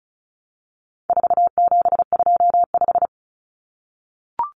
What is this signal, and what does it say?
Morse code, heard as a keyed tone, about 34 wpm, 700 Hz. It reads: 4725
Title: Numbers - 4-Digit 35wpm